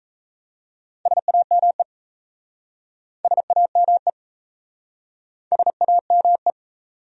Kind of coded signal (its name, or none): Morse code